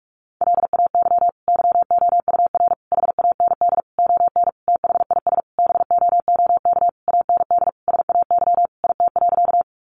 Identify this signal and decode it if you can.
Morse code, heard as a keyed tone, about 44 wpm, 700 Hz. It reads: LAY YOUR HAND ON THIS BOOK AND SAY IT.
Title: Sentences from Top 400 Words 45wpm